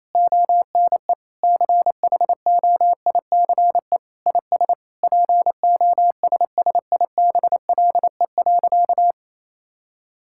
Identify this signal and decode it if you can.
Morse code, heard as a keyed tone, about 28 wpm, 700 Hz. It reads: ONE CHOICE IS POSSIBLE.
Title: Sentences from Top 1000 Words - Review Part 6 - 28wpm